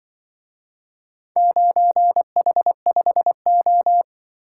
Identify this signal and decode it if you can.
Morse code, heard as a keyed tone, about 22 wpm, 700 Hz. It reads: 9H5O
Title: Speed-Racing - Four Letter-Number Combinations - 20wpm